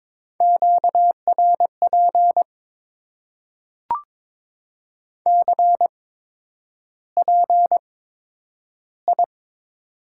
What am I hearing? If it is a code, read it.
Morse code, heard as a keyed tone, about 22 wpm, spaced as for 9 wpm, 700 Hz. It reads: QRP CPI
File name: Top 90 QSO Elements - 8x Character Spacing - 22wpm